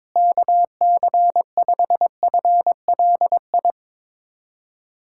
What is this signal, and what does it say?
Morse code, heard as a keyed tone, about 22 wpm, 700 Hz. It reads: KC5FLI
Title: USA Callsign DE Callsign - 6 Characters - 22wpm